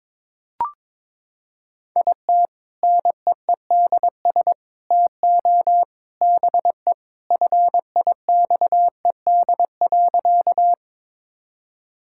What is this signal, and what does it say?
Morse code, heard as a keyed tone, about 22 wpm, 700 Hz. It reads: IT NEEDS TO BE FIXED.
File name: practice / 22wpm / Sentences with 3-Word Phrases - 22wpm